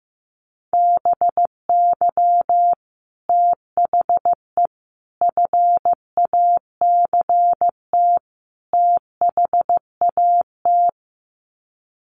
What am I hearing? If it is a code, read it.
Morse code, heard as a keyed tone, about 15 wpm, 700 Hz. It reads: BY THE FACT THAT
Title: Sets of 4 Words 15wpm